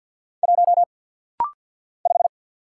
Morse code, heard as a keyed tone, about 49 wpm, 700 Hz. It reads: 1 5
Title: Single Letter-Number - Rapid-Fire 50wpm